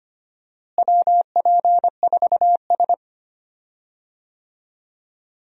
Morse code, heard as a keyed tone, about 25 wpm, 700 Hz. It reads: WP4S
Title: USA Call Signs - 4 Characters - Repeated 3x - 25wpm